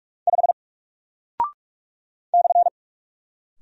Morse code, heard as a keyed tone, about 46 wpm, 700 Hz. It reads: 5 /
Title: Single Character (Letter:Number:Punctuation:Prosign) Alt 45wpm